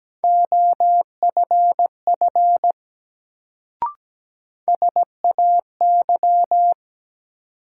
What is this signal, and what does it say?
Morse code, heard as a keyed tone, about 17 wpm, 700 Hz. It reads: OFF SAY
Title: Sentences from Top 200 Words - New Word Drill - 17wpm